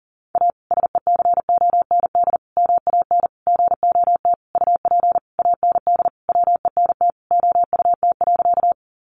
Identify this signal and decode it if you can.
Morse code, heard as a keyed tone, about 40 wpm, 700 Hz. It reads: A SECOND MAN GOT UP AND WENT OUT.
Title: Sentences from Top 300 Words 40wpm